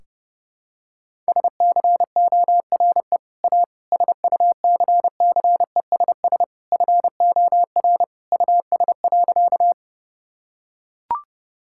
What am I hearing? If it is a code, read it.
Morse code, heard as a keyed tone, about 30 wpm, 700 Hz. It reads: SCORE A SUCCESS FOR US.
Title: Sentences from Top 1600 Words 30wpm